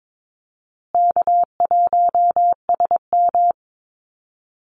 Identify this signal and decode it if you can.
Morse code, heard as a keyed tone, about 22 wpm, 700 Hz. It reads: K1SM
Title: Easy Call Signs Repeated 3x - 22wpm